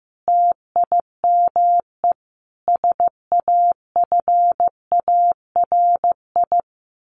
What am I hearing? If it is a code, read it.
Morse code, heard as a keyed tone, about 15 wpm, 700 Hz. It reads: TIME SAFARI
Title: A Sound Of Thunder 15wpm